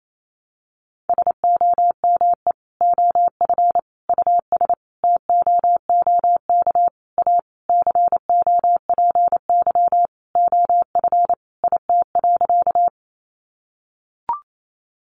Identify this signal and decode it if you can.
Morse code, heard as a keyed tone, about 28 wpm, 700 Hz. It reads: SOME OF US TOOK A COPY OF IT.
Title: Sentences from Top 1400 Words 28wpm